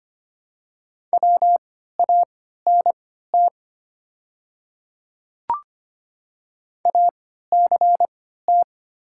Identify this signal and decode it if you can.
Morse code, heard as a keyed tone, about 25 wpm, spaced as for 14 wpm, 700 Hz. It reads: WANT ACT
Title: 3x Character Spacing - Top 200 Words - 25wpm